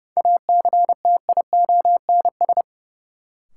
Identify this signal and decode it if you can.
Morse code, heard as a keyed tone, about 30 wpm, 700 Hz. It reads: ACTIONS
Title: Sentences from Top 1000 Words - Review Part 6 - New Word Drill - 30wpm